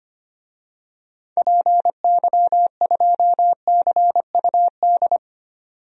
Morse code, heard as a keyed tone, about 25 wpm, 700 Hz. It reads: PY2CUD
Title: DXCC-100 Call Signs - 6 Characters - 25wpm